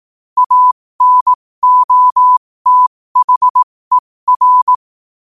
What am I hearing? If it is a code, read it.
Morse code, heard as a keyed tone, about 18 wpm, spaced as for 15 wpm, 1000 Hz. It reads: ANOTHER
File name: qotdfiles / 15wpm